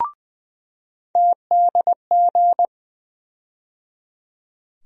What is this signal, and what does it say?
Morse code, heard as a keyed tone, about 20 wpm, 700 Hz. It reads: TDG
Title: USA Call Sign Postfixes - 3 Letters 20wpm